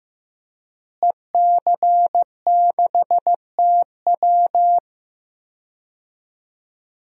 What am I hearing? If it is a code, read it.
Morse code, heard as a keyed tone, about 16 wpm, 700 Hz. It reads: EC6TW
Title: DXCC-100 Call Signs - 5 Characters - Repeated 2x - 15wpm